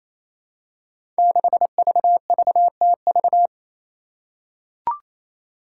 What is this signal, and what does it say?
Morse code, heard as a keyed tone, about 28 wpm, 700 Hz. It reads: 6VVTV
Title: Five Challenging Letter-Number Combinations 28wpm